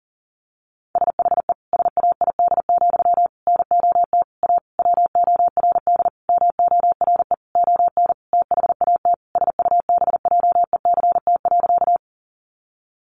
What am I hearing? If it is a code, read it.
Morse code, heard as a keyed tone, about 40 wpm, 700 Hz. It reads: SHE SAID, NOT A WORD MORE ON THAT SUBJECT.
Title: Sentences from Top 800 Words 40wpm